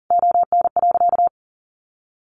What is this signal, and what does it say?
Morse code, heard as a keyed tone, about 37 wpm, 700 Hz. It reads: ON.
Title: Sentences from Top 1100 Words 40wpm